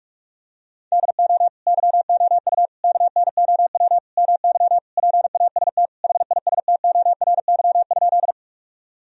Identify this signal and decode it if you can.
Morse code, heard as a keyed tone, about 46 wpm, 700 Hz. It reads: DO YOU KNOW MY PAST HISTORY?
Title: Sentences from Top 1000 Words - Review Part 3 - 45wpm